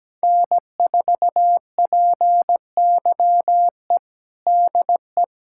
Morse code, heard as a keyed tone, about 17 wpm, 700 Hz. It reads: N4PYE DE
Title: USA Callsign DE Callsign - 5 Characters - 17wpm